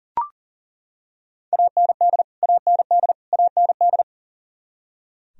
Morse code, heard as a keyed tone, about 40 wpm, 700 Hz. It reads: AND AND AND
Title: Top 100 Words Repeated 40wpm